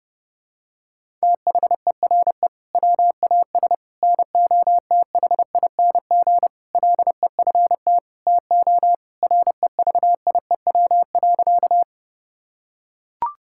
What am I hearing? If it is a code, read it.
Morse code, heard as a keyed tone, about 32 wpm, 700 Hz. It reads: THERE WAS NOTHING LEFT TO REVIEW.
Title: Sentences from Top 1000 Words - Review Part 6 - 30wpm